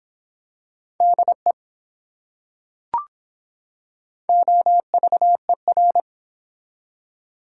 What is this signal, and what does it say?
Morse code, heard as a keyed tone, about 24 wpm, 700 Hz. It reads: DE OVER